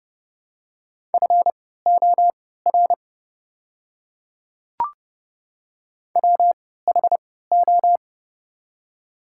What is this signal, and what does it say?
Morse code, heard as a keyed tone, about 29 wpm, spaced as for 17 wpm, 700 Hz. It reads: FOR WHO
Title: 3x Character Spacing - Top 300 Words - 30wpm